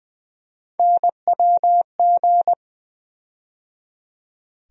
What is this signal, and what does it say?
Morse code, heard as a keyed tone, about 20 wpm, 700 Hz. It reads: NWG
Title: USA Call Sign Postfixes - 3 Letters 20wpm